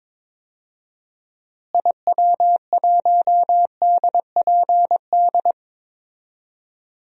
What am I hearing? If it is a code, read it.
Morse code, heard as a keyed tone, about 22 wpm, 700 Hz. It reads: IW1DPD
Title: DXCC-100 Call Signs - 6 Characters - Repeated 2x - 22wpm